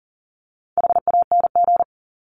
Morse code, heard as a keyed tone, about 40 wpm, 700 Hz. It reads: HANG
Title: Top 1501-1750 Words 40wpm